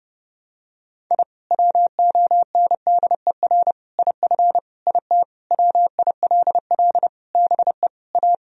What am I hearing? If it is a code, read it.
Morse code, heard as a keyed tone, about 28 wpm, 700 Hz. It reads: I WONDER IF IT WILL BE A
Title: Sentences from Top 1000 Words - Review Part 3 - 30wpm